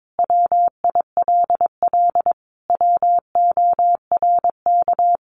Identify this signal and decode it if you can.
Morse code, heard as a keyed tone, about 22 wpm, 700 Hz. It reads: WILL WORK
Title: Sentences from Top 1400 Words 22wpm